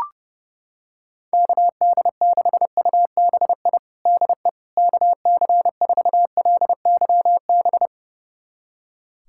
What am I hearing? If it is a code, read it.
Morse code, heard as a keyed tone, about 32 wpm, 700 Hz. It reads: KD6UBI DE KC4LYB
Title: USA Callsign DE Callsign - 6 Characters - 30wpm